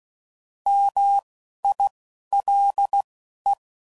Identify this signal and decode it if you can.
Morse code, heard as a keyed tone, about 16 wpm, spaced as for 11 wpm, 800 Hz. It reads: MILE